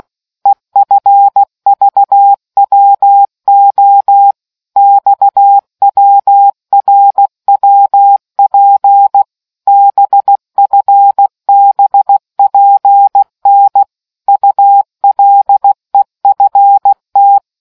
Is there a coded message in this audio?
Yes: Morse code from 0.5 to 17.4 s, heard as a keyed tone, about 16 wpm, 800 Hz. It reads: EFVWO XWRWP BFBPN ULEFT